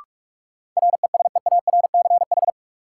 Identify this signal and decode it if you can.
Morse code, heard as a keyed tone, about 46 wpm, 700 Hz. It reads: RESEARCH
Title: Sentences from Top 1000 Words - Review Part 4 - New Word Drill - 45wpm